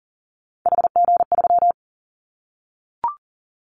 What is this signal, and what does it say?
Morse code, heard as a keyed tone, about 37 wpm, 700 Hz. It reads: HG3
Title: Callsign Prefixes for Top 50 DXCC - 40wpm